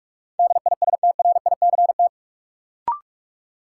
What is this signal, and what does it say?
Morse code, heard as a keyed tone, about 46 wpm, 700 Hz. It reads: DISTRICT
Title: Sentences from Top 1600 Words - New Word Drill - 45wpm